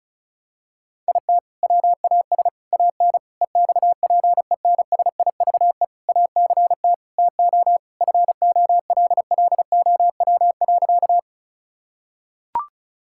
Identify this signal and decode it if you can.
Morse code, heard as a keyed tone, about 36 wpm, 700 Hz. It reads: IT WAS AN EXPENSIVE ACT TO FOLLOW.
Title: Sentences from Top 1000 Words - Review Part 6 - 35wpm